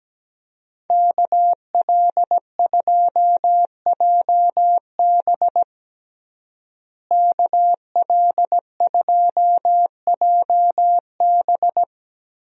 Morse code, heard as a keyed tone, about 17 wpm, 700 Hz. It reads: KL2JB KL2JB
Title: USA Call Signs - 5 Characters - Repeated 3x - 17wpm